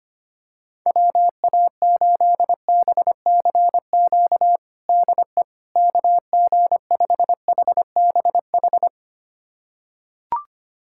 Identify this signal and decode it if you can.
Morse code, heard as a keyed tone, about 25 wpm, 700 Hz. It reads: WA8BCQ DE KG5HBH
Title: USA Callsign DE Callsign - 6 Characters - 25wpm